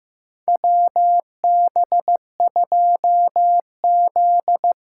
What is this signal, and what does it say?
Morse code, heard as a keyed tone, about 15 wpm, 700 Hz. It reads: WB2Z
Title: USA Call Signs - 8x Character Spacing - 15wpm